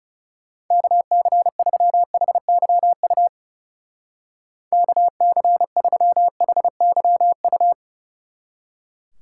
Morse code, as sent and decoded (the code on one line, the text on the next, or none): -.- -.-. ...-- .... -.-- ..- / -.- -.-. ...-- .... -.-- ..-
KC3HYU KC3HYU